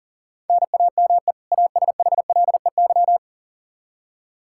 Morse code, heard as a keyed tone, about 40 wpm, 700 Hz. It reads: NAME ASHLEY
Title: Practice QSO - Exchange 1 - Beginning - 40wpm